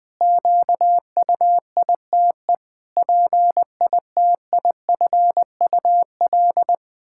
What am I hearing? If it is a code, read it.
Morse code, heard as a keyed tone, about 21 wpm, 700 Hz. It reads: QUITE PITIFUL
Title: Pangram Challenge v2 - 20wpm